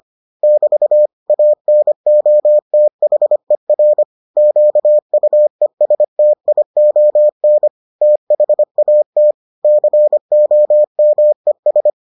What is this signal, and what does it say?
Morse code, heard as a keyed tone, about 25 wpm, 600 Hz. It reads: <BT> ANOTHER QUESTION THAT COMES